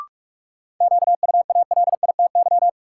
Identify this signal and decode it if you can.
Morse code, heard as a keyed tone, about 46 wpm, 700 Hz. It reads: QUALITY
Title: Sentences from Top 1000 Words - Review Part 4 - New Word Drill - 45wpm